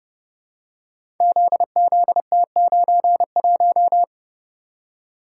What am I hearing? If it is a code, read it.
Morse code, heard as a keyed tone, about 30 wpm, 700 Hz. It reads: ZZT91
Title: Code Groups - 5 Letter:Numbers - 30wpm